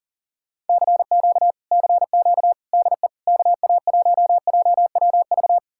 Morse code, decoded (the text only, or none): CQ CQ DE KA1JWV